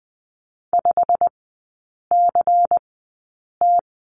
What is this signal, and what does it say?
Morse code, heard as a keyed tone, about 20 wpm, 700 Hz. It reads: 5 C T
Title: 114 Lesson 19e - Letter w - Three Characters 20wpm